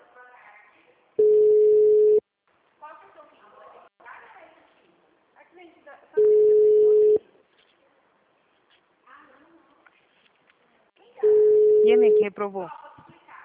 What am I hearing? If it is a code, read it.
Ringback tone (Europe, 425 Hz), with speech.
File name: out- -2016- - - .55271